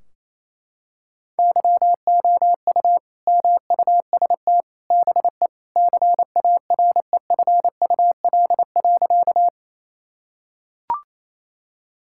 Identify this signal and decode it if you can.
Morse code, heard as a keyed tone, about 28 wpm, 700 Hz. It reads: YOU MUST BE CAREFUL.
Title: Sentences from Top 1600 Words 28wpm